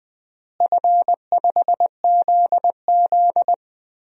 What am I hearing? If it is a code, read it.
Morse code, heard as a keyed tone, about 20 wpm, 700 Hz. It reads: F5ZZ
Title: DXCC-100 Call Signs - 4 Characters - Repeated 3x - 20wpm